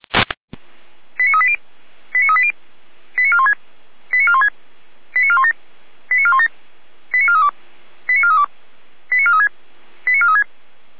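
Five-tone selective call (ZVEI), heard as quick runs of five tones, tones of about 70 ms. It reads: 89380 89380 89416 89416 89417 89417 89432 89432 89436 89436
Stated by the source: ZVEI five-tone selective call received by SDR, from feuerwehr-orlamuende.org fire brigade audio archive.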